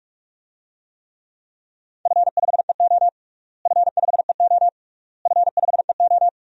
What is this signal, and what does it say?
Morse code, heard as a keyed tone, about 46 wpm, 700 Hz. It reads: F5EO F5EO F5EO
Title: DXCC-100 Call Signs - 4 Characters - Repeated 3x - 45wpm